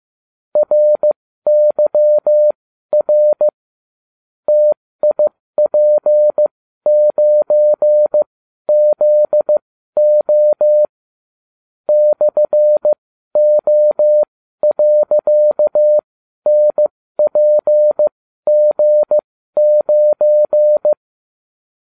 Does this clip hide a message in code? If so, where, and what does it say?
Yes: Morse code from 0.6 to 20.9 s, heard as a keyed tone, about 15 wpm, spaced as for 12 wpm, 600 Hz. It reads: RYR TIP9ZO /O.NPG9